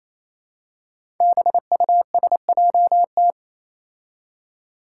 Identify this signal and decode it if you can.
Morse code, heard as a keyed tone, about 28 wpm, 700 Hz. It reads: BUSJT
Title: Five Challenging Letter-Number Combinations 28wpm